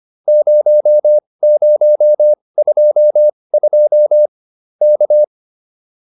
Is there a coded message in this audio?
Yes: Morse code from 0.3 to 5.2 s, heard as a keyed tone, about 25 wpm, spaced as for 20 wpm, 600 Hz. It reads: 0022 K